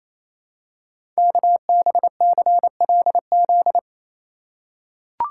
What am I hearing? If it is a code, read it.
Morse code, heard as a keyed tone, about 29 wpm, 700 Hz. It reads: KBCLZ
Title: Code Groups - 5 Letters - 28wpm